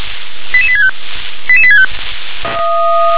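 Five-tone selective call (ZVEI), heard as quick runs of five tones, tones of about 70 ms. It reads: 88865 88865
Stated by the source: ZVEI five-tone selective call 88842 received on SDR.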